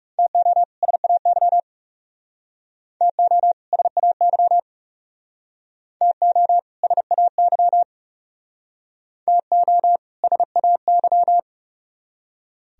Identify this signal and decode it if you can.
Morse code, heard as a keyed tone, about 36 wpm, 700 Hz. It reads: TO SAY TO SAY TO SAY TO SAY
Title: Speed-Racing - Sets of 2 Words 30wpm